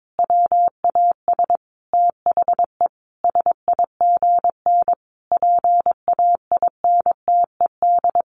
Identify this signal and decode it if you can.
Morse code, heard as a keyed tone, about 22 wpm, 700 Hz. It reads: WAS THE SIGN PAINTED
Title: A Sound Of Thunder 22wpm